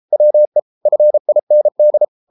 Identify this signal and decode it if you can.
Morse code, heard as a keyed tone, about 33 wpm, 600 Hz. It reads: WE FIND